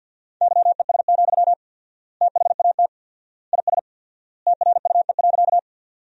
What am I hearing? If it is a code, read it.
Morse code, heard as a keyed tone, about 52 wpm, 700 Hz. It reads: YES, THAT IS TRUE.
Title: 4x Word-Spacing - Sentences from Top 500 Words - 50wpm